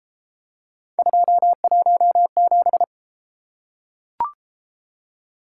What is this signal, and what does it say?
Morse code, heard as a keyed tone, about 29 wpm, 700 Hz. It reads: 217